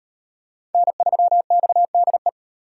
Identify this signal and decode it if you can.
Morse code, heard as a keyed tone, about 38 wpm, 700 Hz. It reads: N3XDE